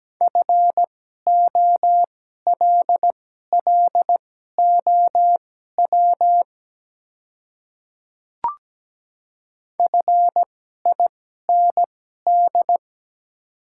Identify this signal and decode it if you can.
Morse code, heard as a keyed tone, about 17 wpm, spaced as for 12 wpm, 700 Hz. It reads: FOLLOW FIND
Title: 2x Character Spacing - Top 200 Words - 17wpm